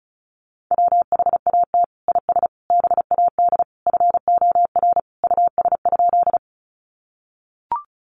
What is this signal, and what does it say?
Morse code, heard as a keyed tone, about 34 wpm, 700 Hz. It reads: WHAT IS BAD FOR US?
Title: Sentences from Top 700 Words 35wpm